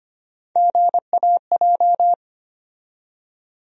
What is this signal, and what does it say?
Morse code, heard as a keyed tone, about 25 wpm, 700 Hz. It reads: GAJ
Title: USA Call Sign Postfixes - 3 Letters 25wpm